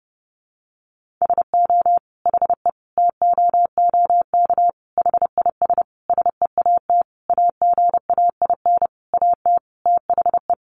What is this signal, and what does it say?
Morse code, heard as a keyed tone, about 30 wpm, 700 Hz. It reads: SO HE TOOK HIS SEAT AGAIN AT THE
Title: Sentences from Top 1500 Words 30wpm